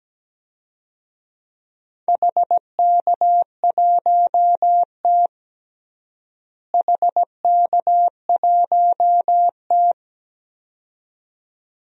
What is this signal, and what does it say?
Morse code, heard as a keyed tone, about 17 wpm, 700 Hz. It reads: HK1T HK1T
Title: DXCC-100 Call Signs - 4 Characters - Repeated 2x - 17wpm